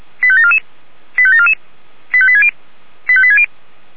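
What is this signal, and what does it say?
Five-tone selective call (ZVEI), heard as quick runs of five tones, tones of about 70 ms. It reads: 85744 85744 85860 85860
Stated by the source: ZVEI five-tone selective call received on SDR.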